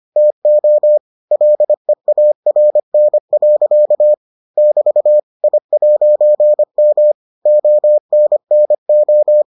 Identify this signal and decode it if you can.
Morse code, heard as a keyed tone, about 25 wpm, 600 Hz. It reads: TO LEARN. <BT> I'M ONNO